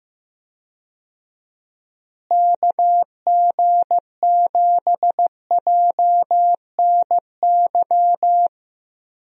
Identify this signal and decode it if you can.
Morse code, heard as a keyed tone, about 16 wpm, 700 Hz. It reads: KG7JNY
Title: USA Call Signs - 6 Characters - Repeated 2x - 15wpm